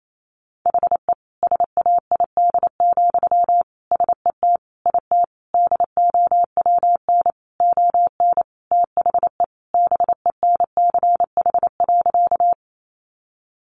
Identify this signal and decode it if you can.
Morse code, heard as a keyed tone, about 28 wpm, 700 Hz. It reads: HE SAID, SET IT DOWN ON THE BENCH.
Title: Sentences from Top 1400 Words 28wpm